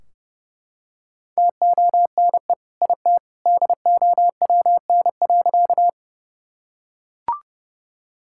Morse code, heard as a keyed tone, about 30 wpm, 700 Hz. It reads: TONE IT DOWN.
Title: Sentences from Top 1600 Words 30wpm